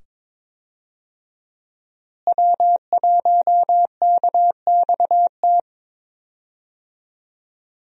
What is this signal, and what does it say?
Morse code, heard as a keyed tone, about 22 wpm, 700 Hz. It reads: W1KXT
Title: USA Call Signs - 5 Characters - Repeated 2x - 22wpm